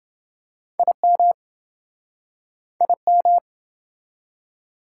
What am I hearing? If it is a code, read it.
Morse code, heard as a keyed tone, about 28 wpm, 700 Hz. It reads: IM IM